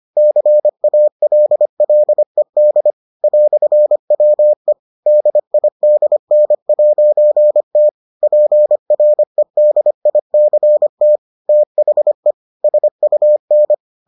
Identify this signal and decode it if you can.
Morse code, heard as a keyed tone, about 25 wpm, 600 Hz. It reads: CALLED "WE DIDN'T PREDICT THE SUN